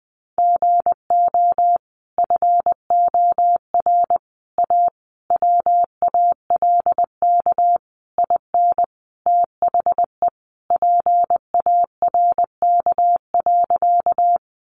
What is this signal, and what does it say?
Morse code, heard as a keyed tone, about 20 wpm, 700 Hz. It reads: GO FOR A WALK IN THE PARK.
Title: Sentences from Top 1300 Words 20wpm